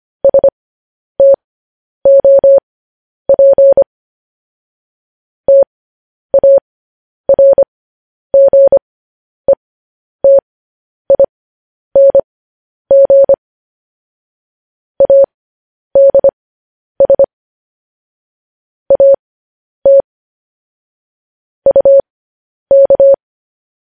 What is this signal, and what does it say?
Morse code, heard as a keyed tone, about 25 wpm, spaced as for 10 wpm, 550 Hz. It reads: STOP TARGETING ADS AT UK
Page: News Headlines in Morse Code at 10 WPM